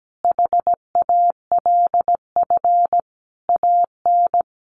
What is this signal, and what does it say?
Morse code, heard as a keyed tone, about 17 wpm, 700 Hz. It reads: HALF AN
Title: Sentences from Top 1100 Words 17wpm